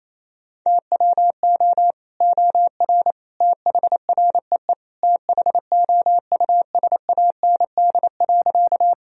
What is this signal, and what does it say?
Morse code, heard as a keyed tone, about 28 wpm, 700 Hz. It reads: TWO OR THREE THOUSAND.
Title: Sentences from Top 1000 Words - Review Part 3 - 28wpm